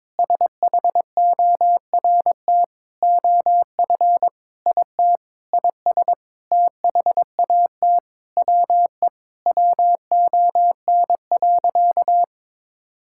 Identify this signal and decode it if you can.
Morse code, heard as a keyed tone, about 22 wpm, 700 Hz. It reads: SHORT OF IT IS THAT WE WON.
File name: practice / 22wpm / Sentences with 8-Word Phrases - 22wpm